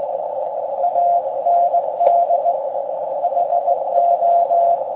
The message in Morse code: AND HO